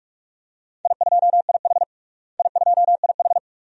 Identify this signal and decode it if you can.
Morse code, heard as a keyed tone, about 46 wpm, 700 Hz. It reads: I2IH I2IH
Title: DXCC-100 Call Signs - 4 Characters - Repeated 2x - 45wpm